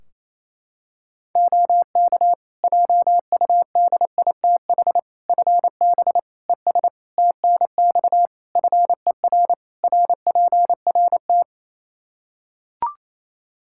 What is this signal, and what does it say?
Morse code, heard as a keyed tone, about 28 wpm, 700 Hz. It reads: OK JUDITH FB ES TNX FER RPRT
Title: Practice QSO - Exchange 2 - Info - 28wpm